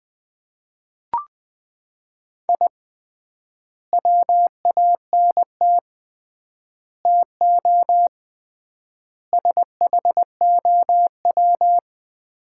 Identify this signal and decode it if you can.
Morse code, heard as a keyed tone, about 21 wpm, 700 Hz. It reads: I WANT TO SHOW
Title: 3x Word-Spacing - Sets of 5 Words - 20wpm